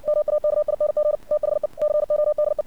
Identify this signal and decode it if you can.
Morse code, heard as a keyed tone, about 53 wpm, 600 Hz. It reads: MAKING THE CODE
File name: archived / CRSnet